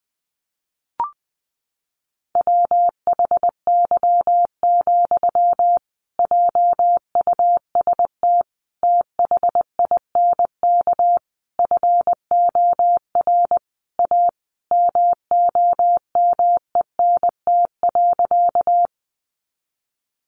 Morse code, heard as a keyed tone, about 20 wpm, 700 Hz. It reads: WHY, JUST THINK FOR A MOMENT.
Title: Sentences from Top 1000 Words 20wpm